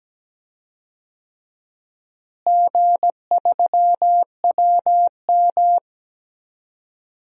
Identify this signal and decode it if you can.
Morse code, heard as a keyed tone, about 17 wpm, 700 Hz. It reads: G3WM
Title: DXCC-100 Call Signs - 4 Characters - Repeated 2x - 17wpm